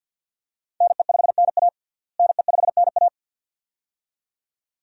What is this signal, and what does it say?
Morse code, heard as a keyed tone, about 52 wpm, 700 Hz. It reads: NE5NA NE5NA
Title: USA Call Signs - 5 Characters - Repeated 2x - 50wpm